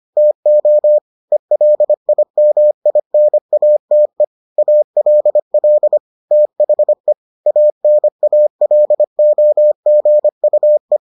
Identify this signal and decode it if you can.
Morse code, heard as a keyed tone, about 25 wpm, 600 Hz. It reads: TO ELIMINATE ALL THE ANALOGUE